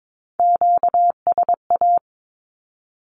Morse code, heard as a keyed tone, about 22 wpm, 700 Hz. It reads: QSA
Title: Q-Codes 22wpm